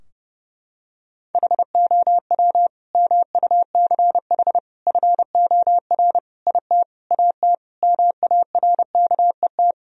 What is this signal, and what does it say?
Morse code, heard as a keyed tone, about 32 wpm, 700 Hz. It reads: HOW MUCH FOR IT AT MARKET
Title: Sentences from Top 1000 Words - Review Part 3 - 30wpm